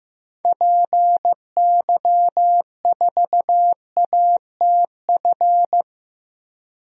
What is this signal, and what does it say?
Morse code, heard as a keyed tone, about 15 wpm, 700 Hz. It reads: PY4ATF
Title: DXCC-100 Call Signs - 6 Characters - 15wpm